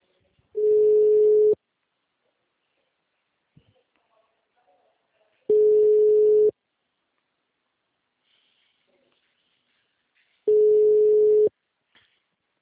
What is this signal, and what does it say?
Ringback tone (Europe, 425 Hz).
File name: out- -2017- - - .22798